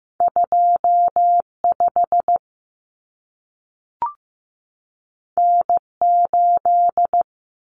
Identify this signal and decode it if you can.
Morse code, heard as a keyed tone, about 15 wpm, 700 Hz. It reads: 25 N8
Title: Common Cut Numbers - 2-Digit 15wpm